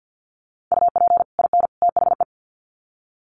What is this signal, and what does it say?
Morse code, heard as a keyed tone, about 50 wpm, 700 Hz. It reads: UP IN THE
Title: Sets of 3 Words 50wpm